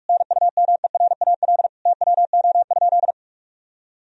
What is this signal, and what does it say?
Morse code, heard as a keyed tone, about 45 wpm, 700 Hz. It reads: NUMERAL TWO?
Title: Sentences from Top 1000 Words - Review Part 2 - 45wpm